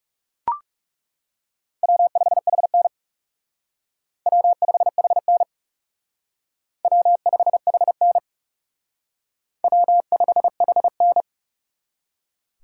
Morse code, heard as a keyed tone, about 39 wpm, 700 Hz. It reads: W5HN W5HN W5HN W5HN